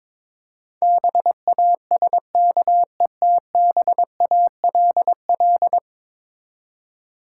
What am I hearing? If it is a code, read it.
Morse code, heard as a keyed tone, about 22 wpm, 700 Hz. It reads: BASKETBALL
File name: Difficult Words with Hints - 22wpm